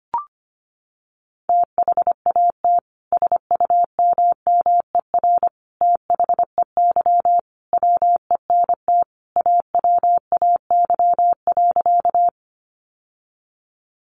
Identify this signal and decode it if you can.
Morse code, heard as a keyed tone, about 25 wpm, 700 Hz. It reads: THAT SUMMER THEY WENT AWAY.
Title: Sentences from Top 1100 Words 25wpm